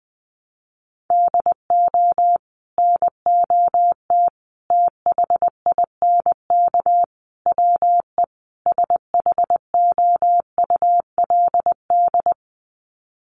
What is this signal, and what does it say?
Morse code, heard as a keyed tone, about 20 wpm, 700 Hz. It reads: DO NOT THINK WE SHOULD
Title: Sets of 5 Words Encore 20wpm